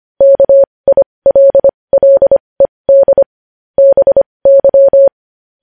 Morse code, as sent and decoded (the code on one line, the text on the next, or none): -.- .. .-.. .-.. -.. / -... -.--
KILLED BY